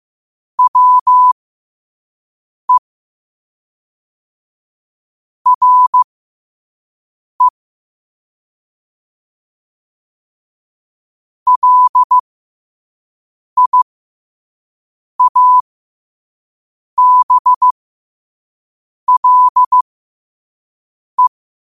Morse code, heard as a keyed tone, about 15 wpm, spaced as for 5 wpm, 1000 Hz. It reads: WE RE LIABLE
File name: qotdfiles / 5wpm